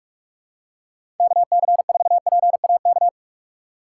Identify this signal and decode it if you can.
Morse code, heard as a keyed tone, about 44 wpm, 700 Hz. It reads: KC4PAK
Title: USA Call Signs - 6 Characters - 6x Character Spacing - 45wpm